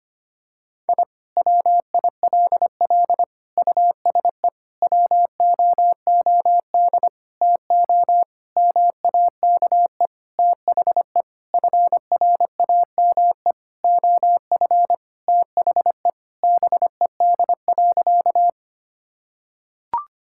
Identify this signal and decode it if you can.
Morse code, heard as a keyed tone, about 25 wpm, 700 Hz. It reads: I WILL USE WOOD TO MAKE THE FRAME OF THE BED.
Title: Sentences from Top 1000 Words - Review Part 6 - 25wpm